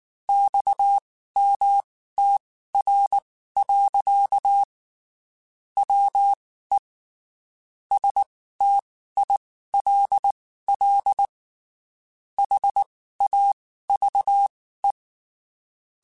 Morse code, heard as a keyed tone, about 19 wpm, spaced as for 14 wpm, 800 Hz. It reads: XMTR. WE STILL HAVE
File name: (spaced) / QSOs Spaced-mp3s / WA7QOX 1 @18 sp